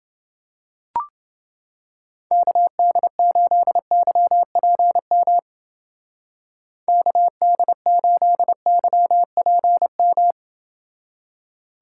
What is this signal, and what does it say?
Morse code, heard as a keyed tone, about 28 wpm, 700 Hz. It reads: KD8YPM KD8YPM